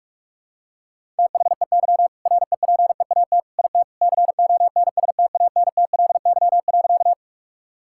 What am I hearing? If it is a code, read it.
Morse code, heard as a keyed tone, about 46 wpm, 700 Hz. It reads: THEY REPEAT IT CONSTANTLY.
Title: Sentences from Top 1600 Words 45wpm